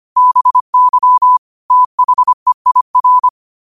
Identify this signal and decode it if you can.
Morse code, heard as a keyed tone, about 25 wpm, 1000 Hz. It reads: BY THEIR
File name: 25wpm